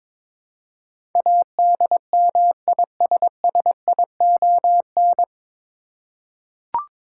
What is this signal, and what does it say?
Morse code, heard as a keyed tone, about 22 wpm, 700 Hz. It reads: ADMISSION
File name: 9 Letter Words - First 3 Letters Given - 22wpm